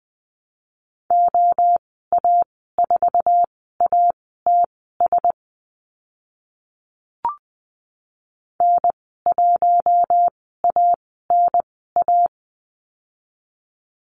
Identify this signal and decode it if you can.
Morse code, heard as a keyed tone, about 20 wpm, spaced as for 14 wpm, 700 Hz. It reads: OA4ATS N1ANA
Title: 062 Lesson 11g - Letter h - Call Signs 20wpm